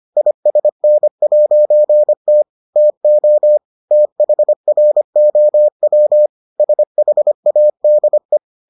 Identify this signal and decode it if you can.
Morse code, heard as a keyed tone, about 25 wpm, 600 Hz. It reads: ISN'T TO THROW SHADE